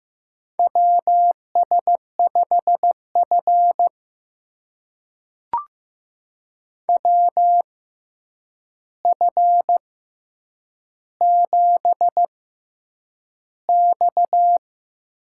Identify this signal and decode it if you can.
Morse code, heard as a keyed tone, about 15 wpm, spaced as for 8 wpm, 700 Hz. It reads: WS5F WF7X
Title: USA Call Signs - 4 Characters - 6x Character Spacing - 15wpm